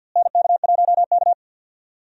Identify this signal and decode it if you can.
Morse code, heard as a keyed tone, about 49 wpm, 700 Hz. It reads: NK1K
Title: USA Call Signs - 4 Characters - Repeated 2x - 50wpm